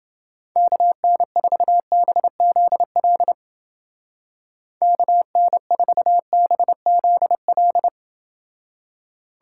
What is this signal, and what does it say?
Morse code, heard as a keyed tone, about 29 wpm, 700 Hz. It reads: KN4BZL KN4BZL